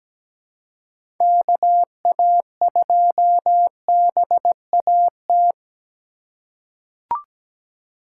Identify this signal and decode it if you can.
Morse code, heard as a keyed tone, about 17 wpm, 700 Hz. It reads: KA2BAT
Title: USA Call Signs - 6 Characters - 2x Character Spacing - 17wpm